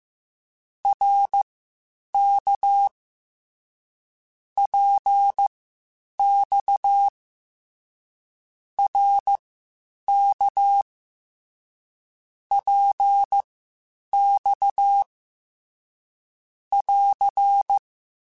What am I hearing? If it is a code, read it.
Morse code, heard as a keyed tone, about 15 wpm, spaced as for 8 wpm, 800 Hz. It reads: RK PX RK PX <AR>
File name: Short Runs / Short Runs-mp3s / Balanced @13